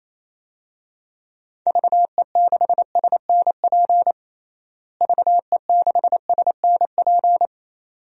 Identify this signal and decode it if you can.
Morse code, heard as a keyed tone, about 28 wpm, 700 Hz. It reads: VE6SNP VE6SNP
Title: DXCC-100 Call Signs - 6 Characters - Repeated 2x - 28wpm